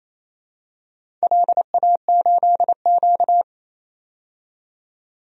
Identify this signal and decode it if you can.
Morse code, heard as a keyed tone, about 29 wpm, 700 Hz. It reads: LA8Q
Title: DXCC-100 Call Signs - 4 Characters - 28wpm